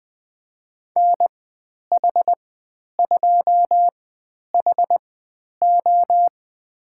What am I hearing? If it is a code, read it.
Morse code, heard as a keyed tone, about 20 wpm, spaced as for 10 wpm, 700 Hz. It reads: NH2HO
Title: 095 Lesson 16g - Letter c - Call Signs 20wpm